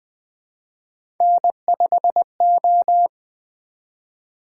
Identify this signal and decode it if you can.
Morse code, heard as a keyed tone, about 20 wpm, 700 Hz. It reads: N5O
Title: Three Letter-Number Combinations - 20wpm